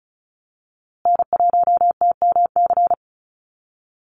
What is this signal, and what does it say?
Morse code, heard as a keyed tone, about 34 wpm, 700 Hz. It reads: N1TMC
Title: USA Call Signs Repeated 2x 35wpm